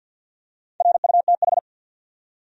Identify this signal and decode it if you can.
Morse code, heard as a keyed tone, about 49 wpm, 700 Hz. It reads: RUTH
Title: USA Common Names - Max 4 Letters - 50wpm